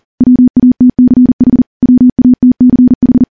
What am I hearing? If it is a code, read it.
Morse code, heard as a keyed tone, about 40 wpm, 250 Hz. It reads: WATCH WATCH
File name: morse / morse-training-40WPM